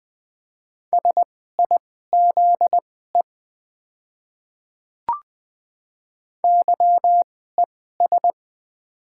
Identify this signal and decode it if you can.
Morse code, heard as a keyed tone, about 20 wpm, spaced as for 15 wpm, 690 Hz. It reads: SIZE YES
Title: 2x Character Spacing - Top 500 Words - 20wpm